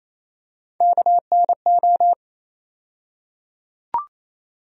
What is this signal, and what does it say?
Morse code, heard as a keyed tone, about 28 wpm, 700 Hz. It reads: KNO
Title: USA Call Sign Postfixes - 3 Letters 28wpm